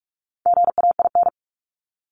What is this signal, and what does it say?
Morse code, heard as a keyed tone, about 44 wpm, 700 Hz. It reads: GAIN
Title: Top 751-1000 Words 45wpm